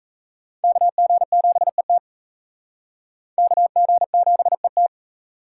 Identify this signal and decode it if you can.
Morse code, heard as a keyed tone, about 34 wpm, 700 Hz. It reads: KG7ET KG7ET
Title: Speed Racing - USA 2x2 Call Signs 28wpm